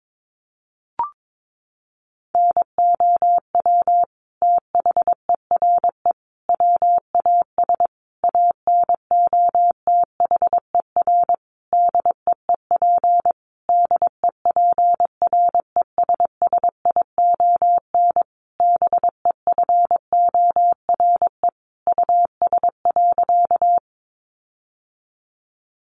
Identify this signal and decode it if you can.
Morse code, heard as a keyed tone, about 22 wpm, 700 Hz. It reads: NOW THERE WAS ANOTHER DEEP DEPRESSION BEFORE US.
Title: Sentences from Top 1300 Words 22wpm